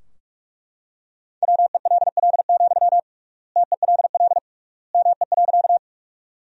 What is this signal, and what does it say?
Morse code, heard as a keyed tone, about 43 wpm, 700 Hz. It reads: WELL, TELL ME.
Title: 3x Word-Spacing - Sentences from Top 200 Words - 45wpm